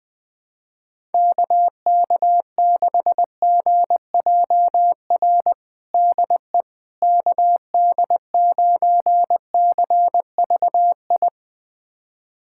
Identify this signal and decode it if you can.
Morse code, heard as a keyed tone, about 20 wpm, 700 Hz. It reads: KK6GJR DE KD9CVI
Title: USA Callsign DE Callsign - 6 Characters - 20wpm